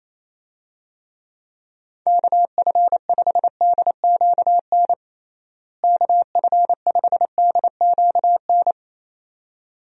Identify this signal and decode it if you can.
Morse code, heard as a keyed tone, about 27 wpm, 700 Hz. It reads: KF5DQN KF5DQN
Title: USA Call Signs - 6 Characters - Repeated 2x - 28wpm